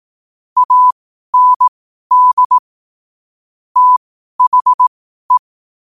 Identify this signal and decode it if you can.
Morse code, heard as a keyed tone, about 18 wpm, spaced as for 12 wpm, 1000 Hz. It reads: AND THE
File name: qotdfiles / 12wpm